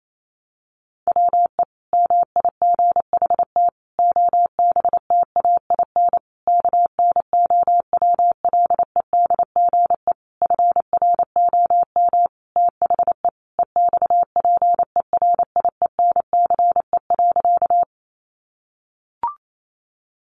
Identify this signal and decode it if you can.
Morse code, heard as a keyed tone, about 28 wpm, 700 Hz. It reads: WE MIGHT OBTAIN KNOWLEDGE FROM THE EXPERIENCE.
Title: Sentences from Top 1300 Words 28wpm